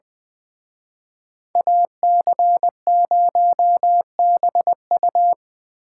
Morse code, heard as a keyed tone, about 24 wpm, 700 Hz. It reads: AC0BU
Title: Speed Racing - USA 2x2 Call Signs 20wpm